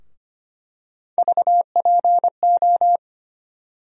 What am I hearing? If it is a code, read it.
Morse code, heard as a keyed tone, about 25 wpm, 700 Hz. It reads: VPO
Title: USA Call Sign Postfixes - 3 Letters 25wpm